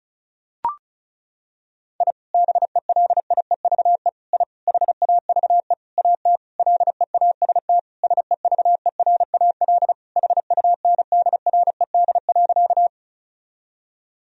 Morse code, heard as a keyed tone, about 36 wpm, 700 Hz. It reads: I BELIEVE I HAVE AT LEAST SEVERAL HUNDRED.
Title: Sentences from Top 1000 Words - Review Part 2 - 35wpm